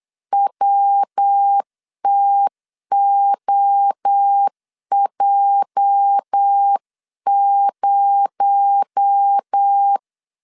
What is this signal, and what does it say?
Morse code, heard as a keyed tone, about 8 wpm, 800 Hz. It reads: WTOJ0
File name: CW / DET / 1 9 1500C S B 8wpm 5624